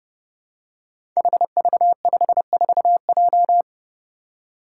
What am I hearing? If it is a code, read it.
Morse code, heard as a keyed tone, about 30 wpm, 700 Hz. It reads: HV54J
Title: Five Challenging Letter-Number Combinations 30wpm